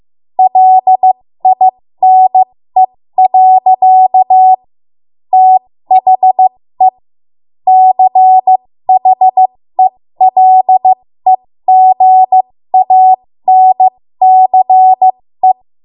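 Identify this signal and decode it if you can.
Morse code, heard as a keyed tone, about 15 wpm, spaced as for 13 wpm, 750 Hz. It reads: LINE. THE CHELEGANCE
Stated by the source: 13 wpm Morse code practice archive transmission.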